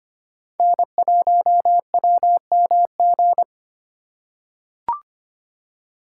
Morse code, heard as a keyed tone, about 25 wpm, 700 Hz. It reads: N1WMG
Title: USA Call Signs - 5 Characters - 4x Character Spacing - 25wpm